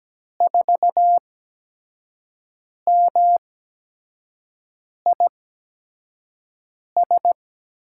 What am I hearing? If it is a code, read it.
Morse code, heard as a keyed tone, about 17 wpm, spaced as for 5 wpm, 700 Hz. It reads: 4MIS
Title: USA Call Signs - 6 Characters - 8x Character Spacing - 17wpm